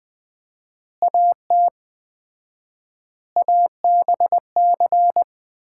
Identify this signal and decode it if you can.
Morse code, heard as a keyed tone, about 20 wpm, 700 Hz. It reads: AT ABC
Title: 4x Word-Spacing - Sets of 5 Words - 20wpm